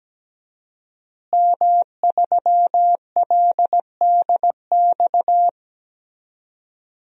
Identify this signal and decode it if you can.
Morse code, heard as a keyed tone, about 17 wpm, 700 Hz. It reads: M3LDX
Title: DXCC-100 Call Signs - 5 Characters - Repeated 2x - 17wpm